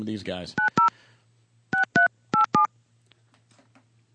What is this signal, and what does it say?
DTMF, heard as telephone touch tones, tones of about 120 ms, with speech. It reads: 90 6387